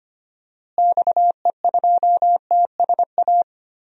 Morse code, heard as a keyed tone, about 25 wpm, 700 Hz. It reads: XE2TSA
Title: DXCC-100 Call Signs - 6 Characters - Repeated 2x - 25wpm